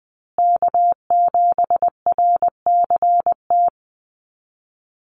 Morse code, heard as a keyed tone, about 20 wpm, 700 Hz. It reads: K7RCT
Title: USA Call Signs Repeated 3x 20wpm